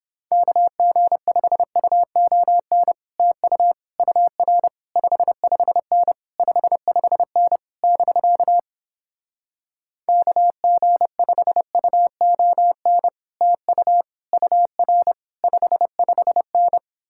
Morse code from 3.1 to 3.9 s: TU

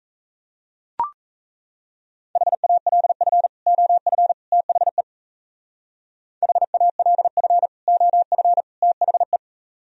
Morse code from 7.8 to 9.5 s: OF THE

